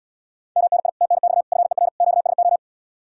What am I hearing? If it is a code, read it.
Morse code, heard as a keyed tone, about 77 wpm, 700 Hz. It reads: MNI TNX FER QSO